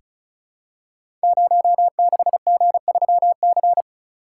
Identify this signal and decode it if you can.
Morse code, heard as a keyed tone, about 36 wpm, 700 Hz. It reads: 06G3C
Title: Code Groups - 5 Letter:Numbers - 35wpm